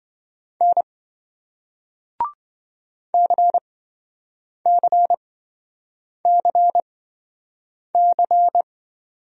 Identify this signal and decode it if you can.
Morse code, heard as a keyed tone, about 28 wpm, spaced as for 7 wpm, 700 Hz. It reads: N CCCC